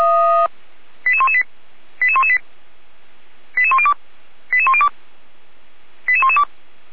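Five-tone selective call (ZVEI), heard as quick runs of five tones, tones of about 80 ms. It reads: 88118 88118 88102 88102 88112
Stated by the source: ZVEI five-tone call 88108 received by SDR, from audio.f85360.feuerwehr-orlamuende.org, fire brigade Orlamünde.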